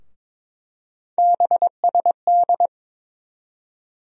Morse code, heard as a keyed tone, about 22 wpm, 700 Hz. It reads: BSD
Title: USA Call Sign Postfixes - 3 Letters 22wpm